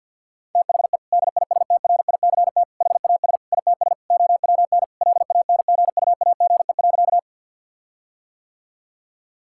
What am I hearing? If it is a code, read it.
Morse code, heard as a keyed tone, about 50 wpm, 700 Hz. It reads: THE DISTRICT HAS ITS OWN LANGUAGE.
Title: Sentences from Top 1600 Words 50wpm